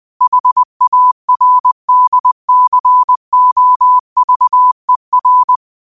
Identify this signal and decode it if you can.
Morse code, heard as a keyed tone, about 20 wpm, 1000 Hz. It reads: HARDCOVER
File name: 20wpm